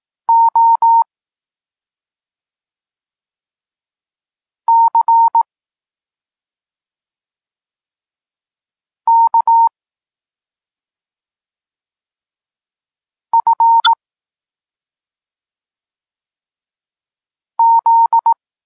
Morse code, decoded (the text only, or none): O C K F Z